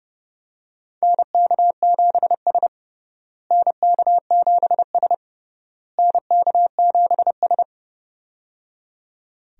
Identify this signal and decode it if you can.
Morse code, heard as a keyed tone, about 30 wpm, 700 Hz. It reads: NK7S NK7S NK7S
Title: USA Call Signs - 4 Characters - Repeated 3x - 30wpm